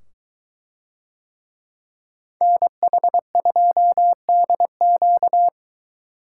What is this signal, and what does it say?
Morse code, heard as a keyed tone, about 22 wpm, 700 Hz. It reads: NH2DQ